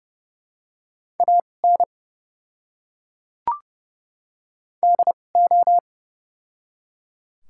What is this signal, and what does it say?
Morse code, heard as a keyed tone, about 32 wpm, spaced as for 5 wpm, 700 Hz. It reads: AND O
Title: 2x Character Spacing - Top 100 Words - 30wpm